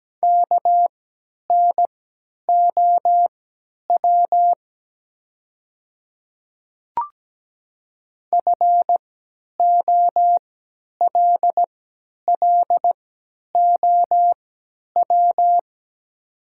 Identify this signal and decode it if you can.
Morse code, heard as a keyed tone, about 17 wpm, spaced as for 10 wpm, 700 Hz. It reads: KNOW FOLLOW
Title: 3x Character Spacing - Top 300 Words - 17wpm